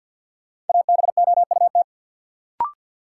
Morse code, heard as a keyed tone, about 53 wpm, 700 Hz. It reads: ABOUT
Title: Sets of 2 Words - New Word Drill - 50wpm